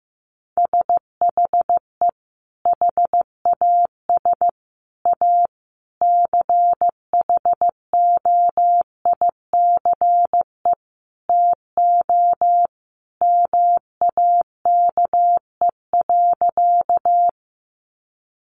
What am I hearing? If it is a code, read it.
Morse code, heard as a keyed tone, about 15 wpm, 700 Hz. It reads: SHE HAS A CHOICE TO MAKE.
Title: Sentences from Top 900 Words 15wpm